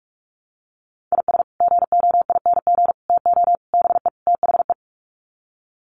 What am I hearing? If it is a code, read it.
Morse code, heard as a keyed tone, about 44 wpm, 700 Hz. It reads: IS GOING TO BE THE
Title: Sets of 5 Words 45wpm